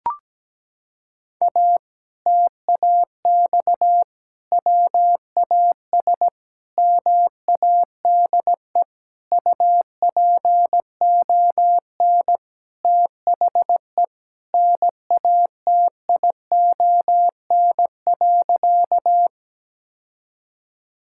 Morse code, heard as a keyed tone, about 17 wpm, 700 Hz. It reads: A TAX WAS MADE UPON THE NATION.
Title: Sentences from Top 1000 Words - Review Part 6 - 17wpm